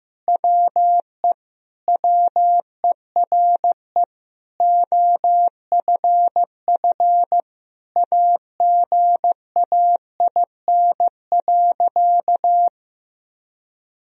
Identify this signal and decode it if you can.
Morse code, heard as a keyed tone, about 16 wpm, 700 Hz. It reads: WE WERE OFF AGAIN.
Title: Sentences from Top 1000 Words - Review Part 2 - 15wpm